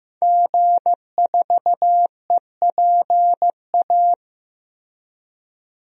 Morse code, heard as a keyed tone, about 15 wpm, 700 Hz. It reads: G4EPA
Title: DXCC-100 Call Signs - 5 Characters - Repeated 2x - 15wpm